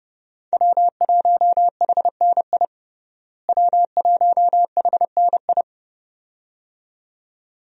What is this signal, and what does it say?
Morse code, heard as a keyed tone, about 32 wpm, 700 Hz. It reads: W1HNI W1HNI
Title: USA Call Signs - 5 Characters - Repeated 2x - 30wpm